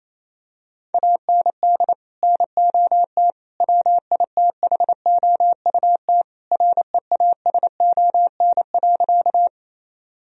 Morse code, heard as a keyed tone, about 28 wpm, 700 Hz. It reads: AND NOT WITHOUT REASON.
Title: Sentences from Top 1000 Words - Review Part 3 - 28wpm